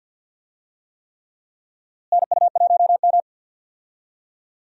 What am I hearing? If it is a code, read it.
Morse code, heard as a keyed tone, about 49 wpm, 700 Hz. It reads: NU1M
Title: USA Call Signs - 4 Characters - Repeated 3x - 50wpm